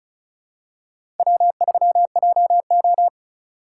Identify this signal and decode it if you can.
Morse code, heard as a keyed tone, about 34 wpm, 700 Hz. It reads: W3JO
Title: USA Call Signs - 4 Characters - 8x Character Spacing - 35wpm